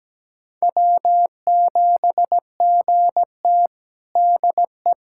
Morse code, heard as a keyed tone, about 17 wpm, 700 Hz. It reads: W7GT DE
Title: USA Callsign DE Callsign - 4 Characters - 17wpm